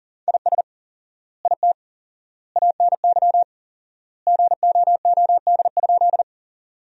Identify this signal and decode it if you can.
Morse code, heard as a keyed tone, about 40 wpm, 700 Hz. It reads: IS IT ANY GOOD?
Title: 4x Word-Spacing - Sentences from Top 200 Words - 40wpm